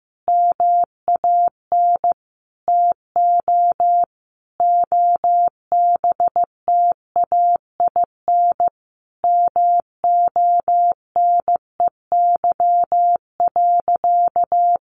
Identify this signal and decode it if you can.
Morse code, heard as a keyed tone, about 15 wpm, 700 Hz. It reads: MAN TO OBTAIN MONEY.
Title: Sentences from Top 1300 Words 15wpm